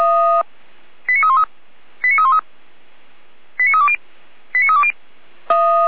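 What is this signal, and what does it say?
Five-tone selective call (ZVEI), heard as quick runs of five tones, tones of about 70 ms. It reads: 89313 89313 89320 89320
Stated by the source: ZVEI five-tone selective call received by SDR, fire brigade Orlamünde (feuerwehr-orlamuende) server.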